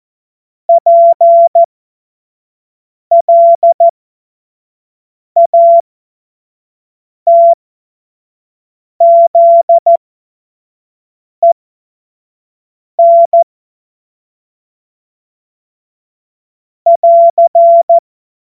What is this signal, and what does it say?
Morse code, heard as a keyed tone, about 14 wpm, spaced as for 5 wpm, 680 Hz. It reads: PLATZEN <AR>
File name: Morsekurs / Witze / Witz 08. August 2023 25 F70